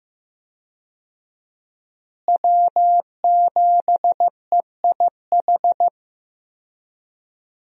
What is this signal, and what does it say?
Morse code, heard as a keyed tone, about 15 wpm, 700 Hz. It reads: W7EIH
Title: USA Call Signs - 5 Characters - Repeated 2x - 15wpm